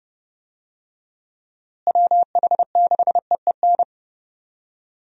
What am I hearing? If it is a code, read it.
Morse code, heard as a keyed tone, about 32 wpm, 700 Hz. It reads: WH6EEN
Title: USA Call Signs - 6 Characters - Repeated 2x - 30wpm